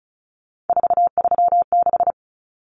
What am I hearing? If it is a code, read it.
Morse code, heard as a keyed tone, about 34 wpm, 700 Hz. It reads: 436
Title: Numbers - 3-Digit 35wpm